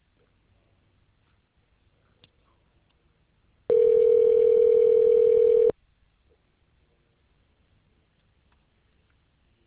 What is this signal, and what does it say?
Ringback tone (North America), 440 + 480 Hz.